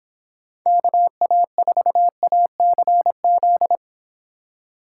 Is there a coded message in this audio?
Yes: Morse code from 0.7 to 3.8 s, heard as a keyed tone, about 26 wpm, 700 Hz. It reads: KA4ACZ